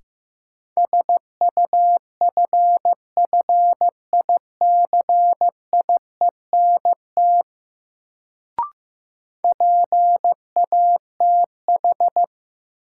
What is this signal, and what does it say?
Morse code, heard as a keyed tone, about 15 wpm, 700 Hz. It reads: SUFFICIENT PATH
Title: Sentences from Top 1400 Words - New Word Drill - 15wpm